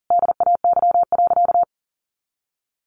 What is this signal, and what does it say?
Morse code, heard as a keyed tone, about 40 wpm, 700 Hz. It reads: DAY.
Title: Sentences from Top 1300 Words 40wpm